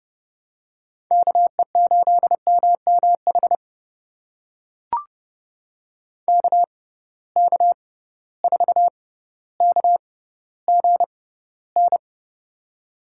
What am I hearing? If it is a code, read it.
Morse code, heard as a keyed tone, about 30 wpm, spaced as for 15 wpm, 700 Hz. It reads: KE8MMH KK4KGN
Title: USA Call Signs - 6 Characters - 6x Character Spacing - 30wpm